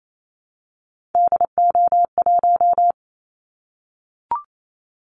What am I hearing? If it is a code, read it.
Morse code, heard as a keyed tone, about 28 wpm, 700 Hz. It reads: DO1
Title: Callsign Prefixes for Top 25 DXCC - Repeated 3x - 28wpm